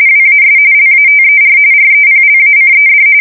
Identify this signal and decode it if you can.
RTTY, about 45.45 baud. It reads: WAS LATER REPLACED
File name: fsk170 / USS-Massachusetts